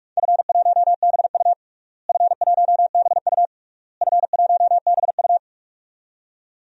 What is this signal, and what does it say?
Morse code, heard as a keyed tone, about 41 wpm, 690 Hz. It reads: F1BU F1BU F1BU
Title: DXCC-100 Call Signs - 4 Characters - Repeated 3x - 45wpm